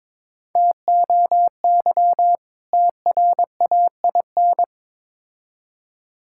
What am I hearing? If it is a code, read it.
Morse code, heard as a keyed tone, about 22 wpm, 700 Hz. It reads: TOY TRAIN
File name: Difficult Words with Hints - 22wpm